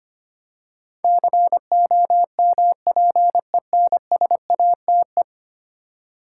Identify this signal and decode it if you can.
Morse code, heard as a keyed tone, about 25 wpm, 700 Hz. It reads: COMPENSATE
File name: 10 Letter Words - First 3 Letters Given - 25wpm